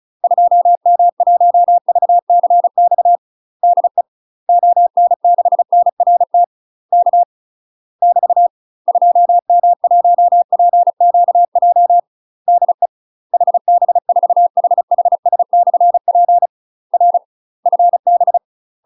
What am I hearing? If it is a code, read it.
Morse code, heard as a keyed tone, about 36 wpm, 700 Hz. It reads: 2M1VCX DE ON6NRT K <BT> 2M1PQJ DE HB4HHS/P R FB
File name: UK MASTER QSOs TRAINING / 35WPM / UK MASTER QSOs 0033